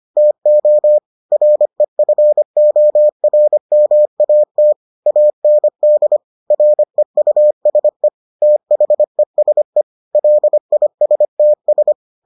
Morse code, heard as a keyed tone, about 25 wpm, 600 Hz. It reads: TO REFORMAT AND REUSE THESE LISTS